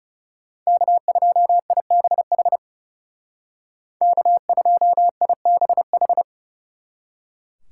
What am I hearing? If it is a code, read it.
Morse code, heard as a keyed tone, about 38 wpm, 700 Hz. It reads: K2IBH K2IBH